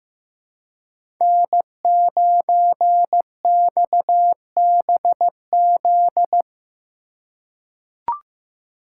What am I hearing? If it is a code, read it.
Morse code, heard as a keyed tone, about 15 wpm, 700 Hz. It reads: N9XBZ
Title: USA Call Signs - 3x Character Spacing - 15wpm